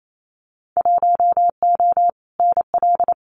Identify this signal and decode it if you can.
Morse code, heard as a keyed tone, about 28 wpm, 700 Hz. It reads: 1O NL
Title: Winter Field Day Exchanges Encore 28wpm